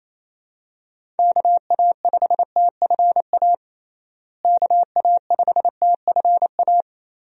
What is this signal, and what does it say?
Morse code, heard as a keyed tone, about 27 wpm, 700 Hz. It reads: KA5TFA KA5TFA
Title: USA Call Signs - 6 Characters - Repeated 2x - 28wpm